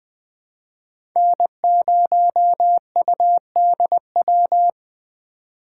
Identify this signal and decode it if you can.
Morse code, heard as a keyed tone, about 20 wpm, 700 Hz. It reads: N0UDW
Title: USA Call Signs - 5 Characters - 4x Character Spacing - 20wpm